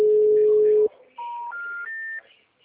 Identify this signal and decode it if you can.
Special information tone, with speech.
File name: out- -2008- - - .58587